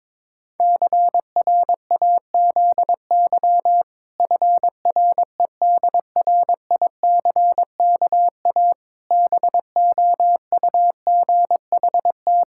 Morse code, heard as a keyed tone, about 22 wpm, 700 Hz. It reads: CRAZY FREDRICKA BOUGHT
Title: Pangram Challenge v2 - 22wpm